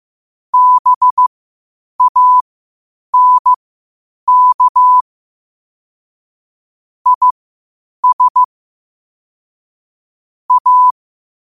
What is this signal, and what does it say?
Morse code, heard as a keyed tone, about 15 wpm, spaced as for 8 wpm, 1000 Hz. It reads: BANK IS A
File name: qotdfiles / 8wpm